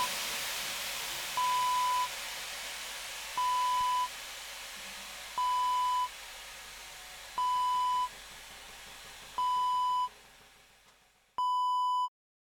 There is no code.